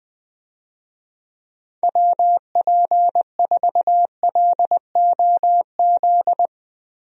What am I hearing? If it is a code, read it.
Morse code, heard as a keyed tone, about 20 wpm, 700 Hz. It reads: WP4LOZ
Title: USA Call Signs - 6 Characters - 20wpm